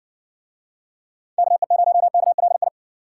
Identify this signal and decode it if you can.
Morse code, heard as a keyed tone, about 67 wpm, 700 Hz. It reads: KE0KLI